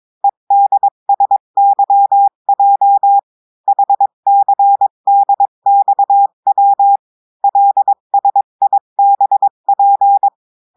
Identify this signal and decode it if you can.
Morse code, heard as a keyed tone, about 22 wpm, 800 Hz. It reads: EDSYJ HCDXW LSIBP
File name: cours-du-mercredi / audios / cours-853-du-2-mai-2024-etxte chiffre et texte clair-a-20-wpm